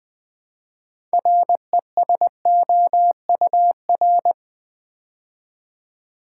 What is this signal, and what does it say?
Morse code, heard as a keyed tone, about 20 wpm, 700 Hz. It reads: RESOUR
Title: Build A Word - 3 Letters At A Time - 8-Letter Words - 20wpm